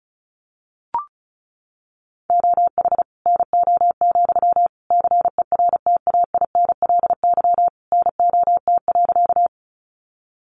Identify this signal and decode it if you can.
Morse code, heard as a keyed tone, about 34 wpm, 700 Hz. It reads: OH NO, CERTAINLY NOT.
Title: Sentences from Top 900 Words 35wpm